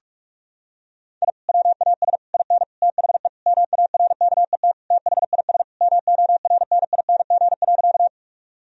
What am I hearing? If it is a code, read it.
Morse code, heard as a keyed tone, about 47 wpm, 700 Hz. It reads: I WAS IN THE MARKET THIS MORNING.
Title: Sentences from Top 1000 Words - Review Part 3 - 45wpm